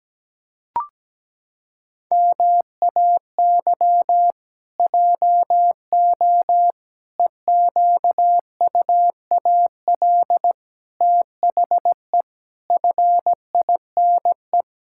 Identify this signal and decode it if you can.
Morse code, heard as a keyed tone, about 17 wpm, 700 Hz. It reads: MAY JO EQUAL THE FINE
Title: Pangram Challenge v2 - 17wpm